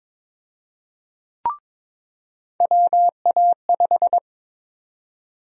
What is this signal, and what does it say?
Morse code, heard as a keyed tone, about 21 wpm, 700 Hz. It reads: WA5
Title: USA Call Sign Prefixes - 22wpm